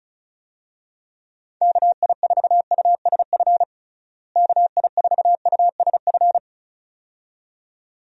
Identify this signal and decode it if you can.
Morse code, heard as a keyed tone, about 34 wpm, 700 Hz. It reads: KI4USF KI4USF
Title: USA Call Signs - 6 Characters - Repeated 2x - 35wpm